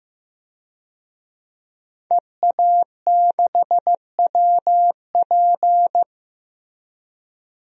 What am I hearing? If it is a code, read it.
Morse code, heard as a keyed tone, about 15 wpm, 700 Hz. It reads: EA6WP
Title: DXCC-100 Call Signs - 5 Characters - Repeated 3x - 15wpm